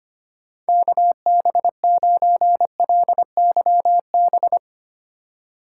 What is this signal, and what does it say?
Morse code, heard as a keyed tone, about 25 wpm, 700 Hz. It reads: KB9LYB
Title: USA Call Signs - 4x Character Spacing - 25wpm